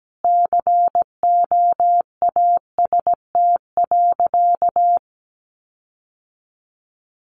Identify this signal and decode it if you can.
Morse code, heard as a keyed tone, about 17 wpm, 700 Hz. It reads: COAST.
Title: Sentences from Top 1300 Words 17wpm